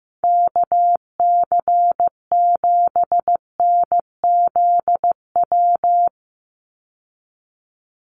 Morse code, heard as a keyed tone, about 15 wpm, 700 Hz. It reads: KC7NZW
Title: USA Call Signs 15wpm